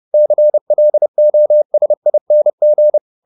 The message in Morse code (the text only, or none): CLOSING